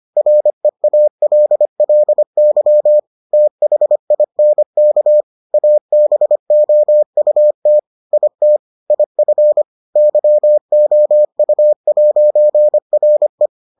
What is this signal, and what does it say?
Morse code, heard as a keyed tone, about 25 wpm, 600 Hz. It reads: REALLY THINK ABOUT IT IF YOU'RE